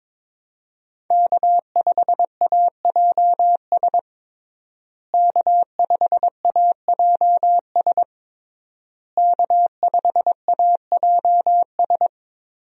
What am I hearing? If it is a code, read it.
Morse code, heard as a keyed tone, about 21 wpm, 700 Hz. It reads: K5AJS K5AJS K5AJS
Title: USA Call Signs - 5 Characters - Repeated 3x - 22wpm